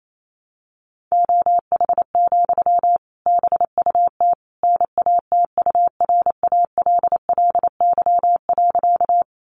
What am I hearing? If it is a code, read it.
Morse code, heard as a keyed tone, about 28 wpm, 700 Hz. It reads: OH, BUT NATURALLY.
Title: Sentences from Top 1300 Words 28wpm